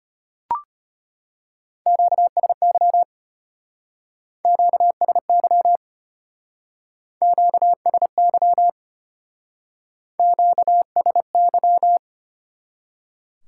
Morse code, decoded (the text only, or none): QSY QSY QSY QSY